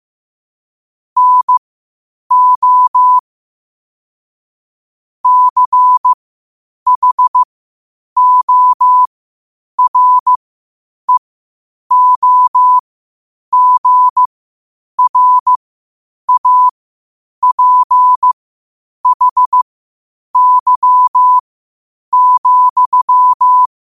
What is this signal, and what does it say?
Morse code, heard as a keyed tone, about 15 wpm, spaced as for 8 wpm, 1000 Hz. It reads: NO CHOREOGRAPHY,
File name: qotdfiles / 8wpm